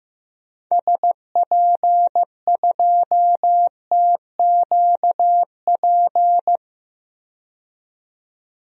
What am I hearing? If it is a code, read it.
Morse code, heard as a keyed tone, about 15 wpm, 700 Hz. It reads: SP2TQP
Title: DXCC-100 Call Signs - 6 Characters - 15wpm